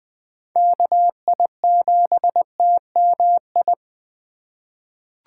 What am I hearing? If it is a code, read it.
Morse code, heard as a keyed tone, about 20 wpm, 700 Hz. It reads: KI7TMI
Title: USA Call Signs - 6 Characters - 3x Character Spacing - 20wpm